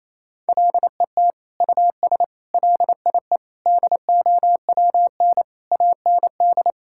Morse code, heard as a keyed tone, about 28 wpm, 700 Hz. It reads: LET US LIE DOWN AND
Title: Sentences from Top 1600 Words 28wpm